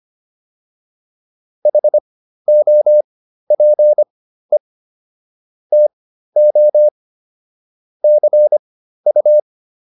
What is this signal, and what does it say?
Morse code, heard as a keyed tone, about 25 wpm, spaced as for 13 wpm, 600 Hz. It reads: HOPE TO CU
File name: PR404 13